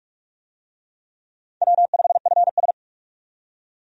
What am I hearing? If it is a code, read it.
Morse code, heard as a keyed tone, about 42 wpm, 700 Hz. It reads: W5FS
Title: USA Call Signs - 4 Characters - 45wpm